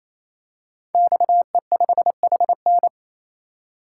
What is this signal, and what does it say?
Morse code, heard as a keyed tone, about 29 wpm, 700 Hz. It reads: XE5HN
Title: Code Groups - 5 Letter:Numbers - 28wpm